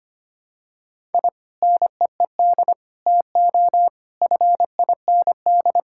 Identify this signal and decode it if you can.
Morse code, heard as a keyed tone, about 25 wpm, 700 Hz. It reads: I NEED TO FIND
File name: practice / 25wpm / Sentences with 3-Word Phrases - 25wpm